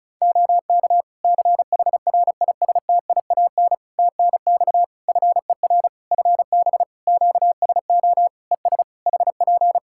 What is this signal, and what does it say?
Morse code, heard as a keyed tone, about 36 wpm, 700 Hz. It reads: OK CHRISTIAN TNX FER FB QSO ES HP
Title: Practice QSO - Exchange 4 - Ending - 35wpm